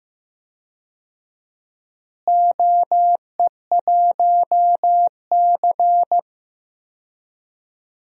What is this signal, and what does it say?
Morse code, heard as a keyed tone, about 15 wpm, 700 Hz. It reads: OE1C
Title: DXCC-100 Call Signs - 4 Characters - Repeated 2x - 15wpm